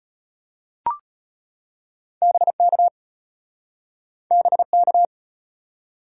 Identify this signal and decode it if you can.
Morse code, heard as a keyed tone, about 33 wpm, 700 Hz. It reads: BK BK